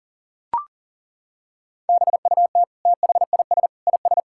Morse code, heard as a keyed tone, about 40 wpm, 700 Hz. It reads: BUT THIS IS
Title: Sentences from Top 1600 Words 40wpm